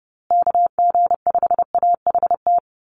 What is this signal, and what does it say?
Morse code, heard as a keyed tone, about 30 wpm, 700 Hz. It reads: KG5AHT
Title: USA Call Signs Repeated 3x 30wpm